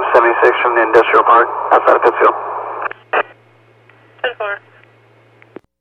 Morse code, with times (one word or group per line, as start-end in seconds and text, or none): none